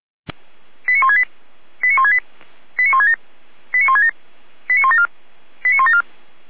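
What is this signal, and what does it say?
Five-tone selective call (ZVEI), heard as quick runs of five tones, tones of about 70 ms. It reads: 89168 89168 89167 89167 89174 89174
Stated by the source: Five-tone ZVEI selective call received by SDR.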